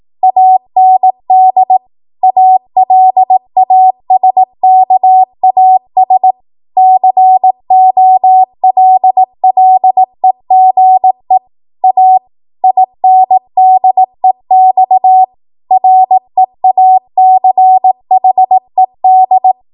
Morse code, heard as a keyed tone, about 18 wpm, 750 Hz. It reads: AND ALASKAS COLLEGE A INDEX REACHED
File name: Archive / 18 WPM / 18WPM